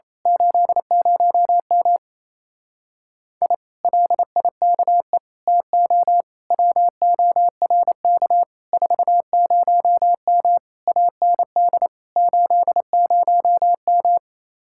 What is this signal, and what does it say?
Morse code, heard as a keyed tone, about 27 wpm, 700 Hz. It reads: 80M I LIKE TO WORK 40M AND 80M
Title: Speed-Racing Common QSO Phrases - 28wpm